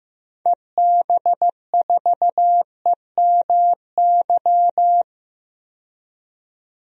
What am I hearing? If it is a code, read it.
Morse code, heard as a keyed tone, about 15 wpm, 700 Hz. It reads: EB4EMY
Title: DXCC-100 Call Signs - 6 Characters - 15wpm